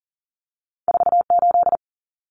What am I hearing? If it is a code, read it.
Morse code, heard as a keyed tone, about 40 wpm, 700 Hz. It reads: 48
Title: Common Cut Numbers - 2-Digit 40wpm